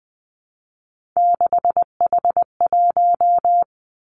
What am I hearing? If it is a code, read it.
Morse code, heard as a keyed tone, about 20 wpm, 700 Hz. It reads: 6H1
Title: Three Challenging Letter-Number Combinations 20wpm